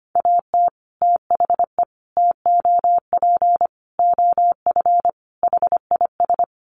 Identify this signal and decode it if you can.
Morse code, heard as a keyed tone, about 25 wpm, 700 Hz. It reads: AT THE TOP OF HIS
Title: Sets of 5 Words Encore 25wpm